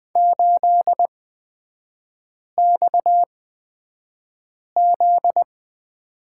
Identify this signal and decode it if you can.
Morse code, heard as a keyed tone, about 20 wpm, spaced as for 5 wpm, 700 Hz. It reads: 8XZ
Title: 243 Lesson 39e - Letter z - Three Characters 20wpm